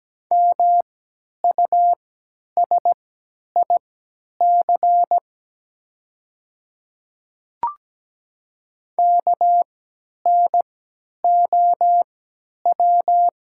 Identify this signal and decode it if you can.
Morse code, heard as a keyed tone, about 17 wpm, spaced as for 10 wpm, 700 Hz. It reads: MUSIC KNOW
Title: 3x Character Spacing - Top 300 Words - 17wpm